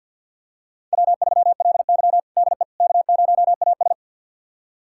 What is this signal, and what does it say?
Morse code, heard as a keyed tone, about 51 wpm, 700 Hz. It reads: W3LY DE K0AS
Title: USA Callsign DE Callsign - 4 Characters - 50wpm